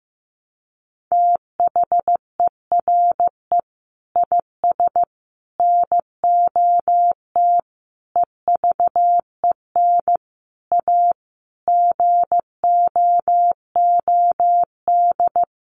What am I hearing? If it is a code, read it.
Morse code, heard as a keyed tone, about 15 wpm, 700 Hz. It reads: THERE IS NOT EVEN A GOOD
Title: Sentences from Top 1500 Words 15wpm